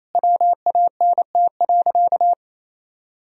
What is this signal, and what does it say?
Morse code, heard as a keyed tone, about 28 wpm, 700 Hz. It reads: WANT.
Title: Sentences from Top 1000 Words - Review Part 3 - 28wpm